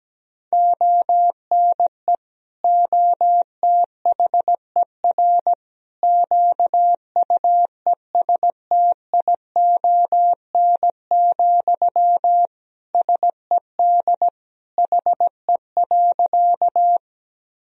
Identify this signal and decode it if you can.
Morse code, heard as a keyed tone, about 17 wpm, 700 Hz. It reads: ONE OTHER QUESTION, SED HE.
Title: Sentences from Top 400 Words and Abbreviations 17wpm